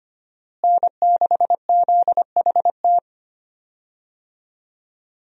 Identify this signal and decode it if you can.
Morse code, heard as a keyed tone, about 25 wpm, 700 Hz. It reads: N6ZHT
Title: USA Call Signs - 5 Characters - Repeated 3x - 25wpm